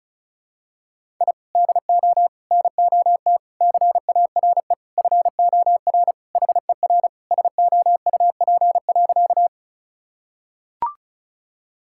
Morse code, heard as a keyed tone, about 36 wpm, 700 Hz. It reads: I DO NOT CARE FOR HER SOUP.
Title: Sentences from Top 1000 Words - Review Part 6 - 35wpm